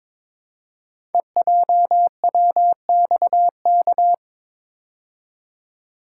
Morse code, heard as a keyed tone, about 22 wpm, 700 Hz. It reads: EJWXK
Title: Code Groups - 5 Letters - 22wpm